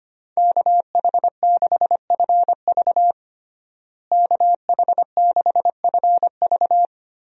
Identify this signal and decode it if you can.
Morse code, heard as a keyed tone, about 25 wpm, 700 Hz. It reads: KH6FV KH6FV
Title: USA Call Signs - 5 Characters - Repeated 2x - 25wpm